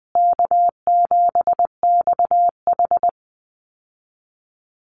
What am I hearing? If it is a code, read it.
Morse code, heard as a keyed tone, about 20 wpm, 700 Hz. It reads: K7XH
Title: Call Signs Repeated 2x - 20wpm